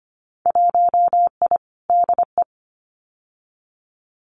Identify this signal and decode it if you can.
Morse code, heard as a keyed tone, about 25 wpm, 700 Hz. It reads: 1I DE
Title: Winter Field Day Exchanges Encore 25wpm